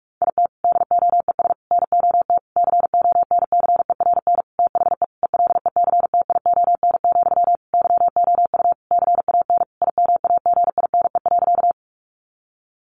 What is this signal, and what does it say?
Morse code, heard as a keyed tone, about 44 wpm, 700 Hz. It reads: IT DOES NOT CONCERN THE ELECTION, YOU CAN IMAGINE.
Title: Sentences from Top 1500 Words 45wpm